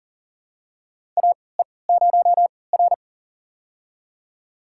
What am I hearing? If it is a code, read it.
Morse code, heard as a keyed tone, about 40 wpm, spaced as for 23 wpm, 700 Hz. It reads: AE0R
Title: USA Call Signs - 4 Characters - 3x Character Spacing - 40wpm